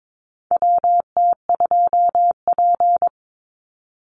Morse code, heard as a keyed tone, about 22 wpm, 700 Hz. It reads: WT2P
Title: Call Signs Repeated 2x - 22wpm